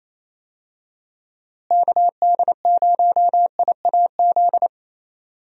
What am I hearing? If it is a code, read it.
Morse code, heard as a keyed tone, about 28 wpm, 700 Hz. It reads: KD0IAZ
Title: USA Call Signs - 6 Characters - Repeated 2x - 28wpm